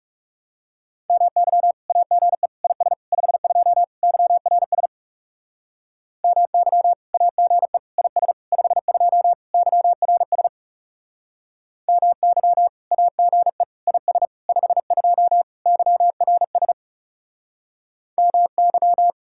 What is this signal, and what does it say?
Morse code, heard as a keyed tone, about 36 wpm, 700 Hz. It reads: MY AGE IS 52 YRS MY AGE IS 52 YRS MY AGE IS 52 YRS MY
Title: Speed-Racing Common QSO Phrases Encore - 30wpm